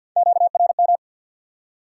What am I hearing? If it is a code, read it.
Morse code, heard as a keyed tone, about 48 wpm, 700 Hz. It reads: QRM
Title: Top 90 QSO Elements 50wpm